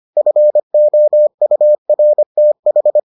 Morse code, heard as a keyed tone, about 25 wpm, 600 Hz. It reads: FOURTH